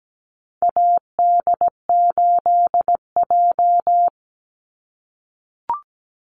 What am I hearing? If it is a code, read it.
Morse code, heard as a keyed tone, about 17 wpm, 700 Hz. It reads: AD8J
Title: Call Signs - 17wpm